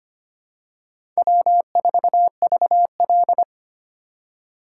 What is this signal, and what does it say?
Morse code, heard as a keyed tone, about 25 wpm, 700 Hz. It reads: W4VL
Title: USA Call Signs - 4x Character Spacing - 25wpm